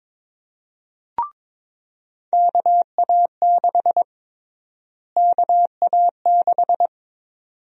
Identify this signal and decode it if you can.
Morse code, heard as a keyed tone, about 22 wpm, 700 Hz. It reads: KA6 KA6
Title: USA Call Sign Prefixes - Repeated 2x - 22wpm